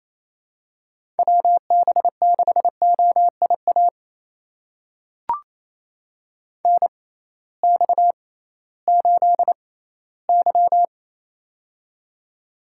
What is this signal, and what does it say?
Morse code, heard as a keyed tone, about 28 wpm, spaced as for 16 wpm, 700 Hz. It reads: WB6OIA NX8Y
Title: USA Call Signs - 6x Character Spacing - 28wpm